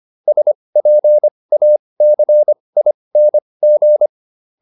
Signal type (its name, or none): Morse code